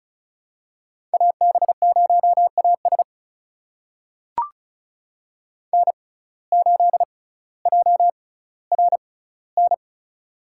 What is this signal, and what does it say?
Morse code, heard as a keyed tone, about 34 wpm, spaced as for 18 wpm, 700 Hz. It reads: AB0AS N8JRN
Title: USA Call Signs - 5 Characters - 6x Character Spacing - 35wpm